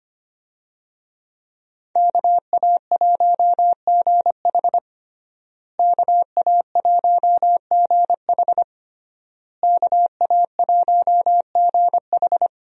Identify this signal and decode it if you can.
Morse code, heard as a keyed tone, about 25 wpm, 700 Hz. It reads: KA1GH KA1GH KA1GH
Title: USA Call Signs - 5 Characters - Repeated 3x - 25wpm